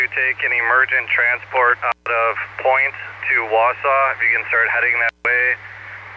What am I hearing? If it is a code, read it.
Emergency Alert System (SAME) tones, heard as harsh buzzing bursts, with speech. It is heard from 1.0 to 5.1 s.